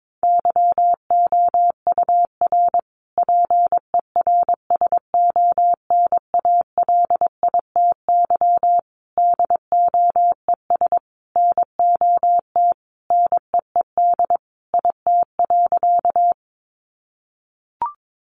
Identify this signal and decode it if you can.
Morse code, heard as a keyed tone, about 22 wpm, 700 Hz. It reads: YOUR PERSONALITY DOES NOT NEED IT.
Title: Sentences from Top 1400 Words 22wpm